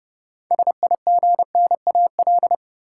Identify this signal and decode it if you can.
Morse code, heard as a keyed tone, about 29 wpm, 700 Hz. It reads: SIGNAL
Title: Six Letter Words - 30wpm